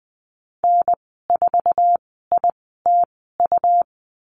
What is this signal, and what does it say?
Morse code, heard as a keyed tone, about 20 wpm, spaced as for 14 wpm, 700 Hz. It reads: N4ITU
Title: 140 Lesson 23g - Letter f - Call Signs 20wpm